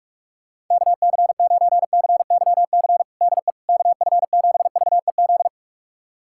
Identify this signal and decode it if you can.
Morse code, heard as a keyed tone, about 46 wpm, 700 Hz. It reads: KC9CYC DE KF7VEZ
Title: USA Callsign DE Callsign 45wpm